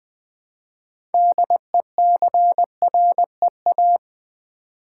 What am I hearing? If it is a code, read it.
Morse code, heard as a keyed tone, about 20 wpm, 700 Hz. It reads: DECREA
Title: Build A Word - 3 Letters At A Time - 8-Letter Words - 20wpm